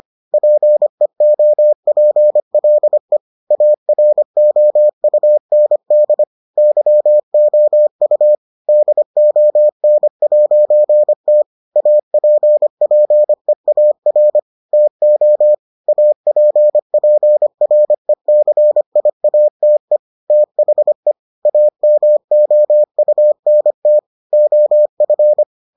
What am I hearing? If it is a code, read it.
Morse code, heard as a keyed tone, about 25 wpm, 600 Hz. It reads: PEOPLE AROUND YOU DON'T APPEAR TO APPRECIATE THE AMOUNT OF